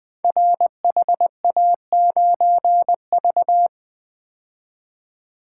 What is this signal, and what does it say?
Morse code, heard as a keyed tone, about 20 wpm, 700 Hz. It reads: RHA9V
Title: Code Groups - 5 Letter:Numbers - 20wpm